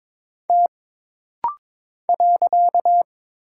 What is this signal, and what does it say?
Morse code, heard as a keyed tone, about 22 wpm, 700 Hz. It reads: T .
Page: This rapid-fire timing is geared to push your ICR (Instant Character Recognition) skills.